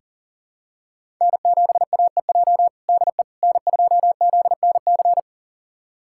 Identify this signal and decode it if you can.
Morse code, heard as a keyed tone, about 40 wpm, 700 Hz. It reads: N7AEJ DE N2ZNC
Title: USA Callsign DE Callsign 40wpm